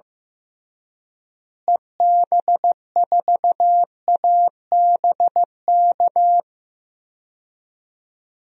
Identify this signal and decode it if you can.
Morse code, heard as a keyed tone, about 16 wpm, 700 Hz. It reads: EB4ABK
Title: DXCC-100 Call Signs - 6 Characters - Repeated 3x - 15wpm